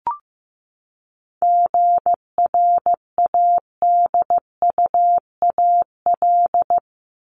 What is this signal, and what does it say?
Morse code, heard as a keyed tone, about 15 wpm, 700 Hz. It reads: GRADUAL
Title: LY Suffix - Top 100 Words with Base Words 15wpm